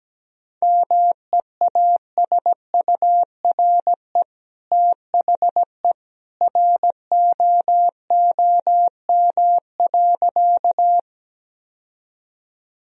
Morse code, heard as a keyed tone, about 17 wpm, 700 Hz. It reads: MEASURE THE ROOM.
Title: Sentences from Top 1000 Words - Review Part 2 - 17wpm